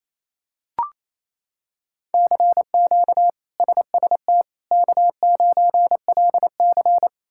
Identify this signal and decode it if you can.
Morse code, heard as a keyed tone, about 28 wpm, 700 Hz. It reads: CQ SST K9LC
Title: K1USN SST QSO Practice - 28wpm